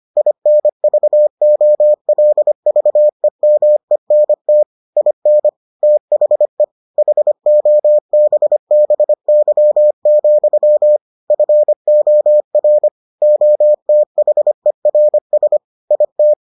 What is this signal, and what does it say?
Morse code, heard as a keyed tone, about 25 wpm, 600 Hz. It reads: INVOLVEMENT IN THE HOBBY, FOR OTHERS IT